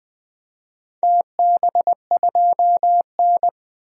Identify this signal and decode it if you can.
Morse code, heard as a keyed tone, about 20 wpm, 700 Hz. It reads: TB2N
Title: DXCC-100 Call Signs - 4 Characters - 20wpm